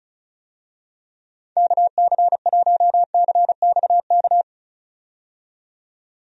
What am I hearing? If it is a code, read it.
Morse code, heard as a keyed tone, about 34 wpm, 700 Hz. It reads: KC1CXK
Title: USA Call Signs - 6 Characters - 35wpm